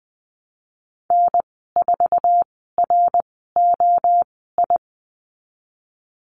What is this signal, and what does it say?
Morse code, heard as a keyed tone, about 20 wpm, spaced as for 14 wpm, 700 Hz. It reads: N4ROI
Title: 062 Lesson 11g - Letter h - Call Signs 20wpm